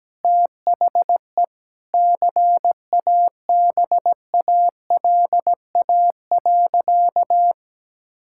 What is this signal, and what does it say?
Morse code, heard as a keyed tone, about 17 wpm, 700 Hz. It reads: THE CABALA.
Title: Pangram Challenge v2 - 17wpm